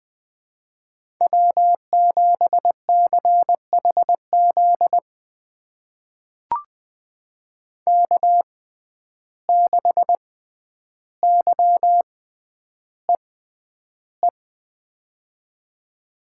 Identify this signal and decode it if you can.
Morse code, heard as a keyed tone, about 20 wpm, spaced as for 10 wpm, 700 Hz. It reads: W7CHZ K6YEE
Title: USA Call Signs - 5 Characters - 6x Character Spacing - 20wpm